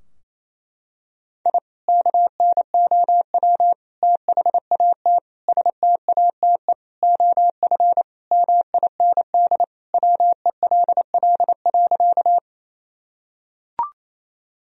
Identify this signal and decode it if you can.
Morse code, heard as a keyed tone, about 28 wpm, 700 Hz. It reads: I KNOW THAT STATE OF MIND WELL.
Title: Sentences from Top 1000 Words - Review Part 2 - 28wpm